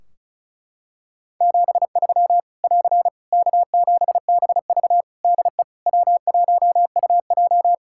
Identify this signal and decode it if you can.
Morse code, heard as a keyed tone, about 34 wpm, 700 Hz. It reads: 73 <AR> K7BV DE W1UJ
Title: K1USN SST QSO Practice 35wpm